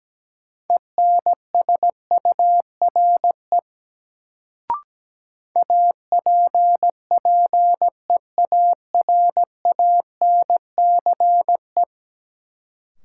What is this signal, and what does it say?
Morse code, heard as a keyed tone, about 17 wpm, 700 Hz. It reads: ENSURE APPEARANCE
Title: Sentences from Top 1100 Words - New Word Drill - 17wpm